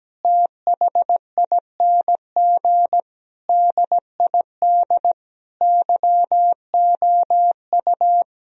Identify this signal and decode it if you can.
Morse code, heard as a keyed tone, about 17 wpm, 700 Hz. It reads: THING DID YOU
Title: Sentences from Top 1000 Words - Review Part 3 - 17wpm